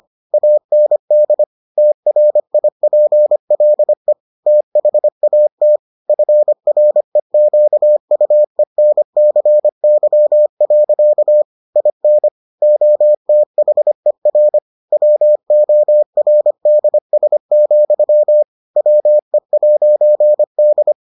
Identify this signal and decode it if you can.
Morse code, heard as a keyed tone, about 25 wpm, 600 Hz. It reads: AND TRIPLE THAT FREQUENCY. IN OTHER WORDS, WE'D